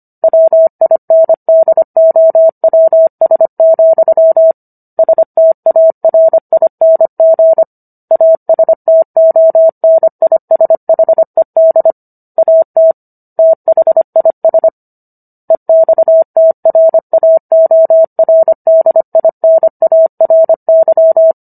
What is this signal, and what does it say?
Morse code, heard as a keyed tone, about 25 wpm, 650 Hz. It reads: WINDOWS, STARING ASTONISHED AT THIS EXTRAORDINARY
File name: Book One / 16-Chapter 16 - 25 WPM